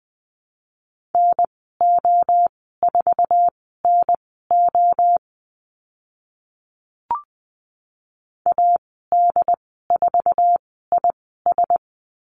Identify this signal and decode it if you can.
Morse code, heard as a keyed tone, about 20 wpm, spaced as for 15 wpm, 700 Hz. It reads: NO4NO AD4IS
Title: 082 Lesson 14g - Number 2 - Call Signs 20wpm